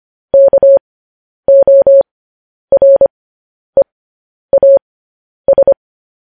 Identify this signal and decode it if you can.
Morse code, heard as a keyed tone, about 25 wpm, spaced as for 10 wpm, 550 Hz. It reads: KOREAS